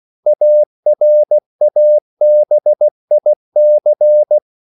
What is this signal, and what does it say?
Morse code, heard as a keyed tone, about 16 wpm, 600 Hz. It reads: ARABIC